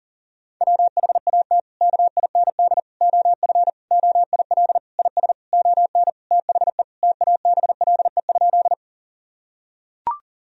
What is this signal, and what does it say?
Morse code, heard as a keyed tone, about 37 wpm, 700 Hz. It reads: WHAT KIND OF OIL IS ON THE TABLE?
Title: Sentences from Top 1000 Words - Review Part 4 - 40wpm